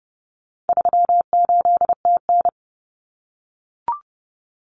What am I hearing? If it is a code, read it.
Morse code, heard as a keyed tone, about 30 wpm, 700 Hz. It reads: 38TN
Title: Common Cut Numbers - 4-Digit 30wpm